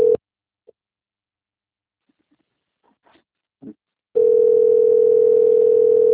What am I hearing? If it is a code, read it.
Ringback tone (North America), 440 + 480 Hz.